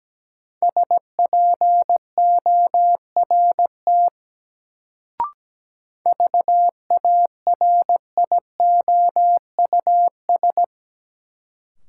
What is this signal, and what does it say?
Morse code, heard as a keyed tone, about 17 wpm, 700 Hz. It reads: SPORT VARIOUS
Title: Sentences from Top 1000 Words - Review Part 4 - New Word Drill - 17wpm